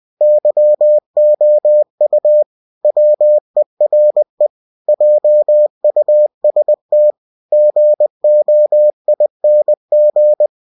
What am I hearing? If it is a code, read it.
Morse code, heard as a keyed tone, about 20 wpm, 600 Hz. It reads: YOU WERE JUST GOING
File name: cw-books / winnie-the-pooh / winnie-20wpm-page-0025